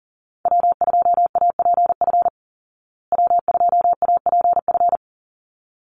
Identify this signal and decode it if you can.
Morse code, heard as a keyed tone, about 40 wpm, 700 Hz. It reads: W2APF W2APF
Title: Call Signs Repeated 3x - 40wpm